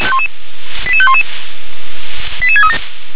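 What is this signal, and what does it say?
Five-tone selective call (ZVEI), heard as quick runs of five tones, tones of about 70 ms. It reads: 88511 88637
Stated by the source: ZVEI five-tone selective call received with an SDR.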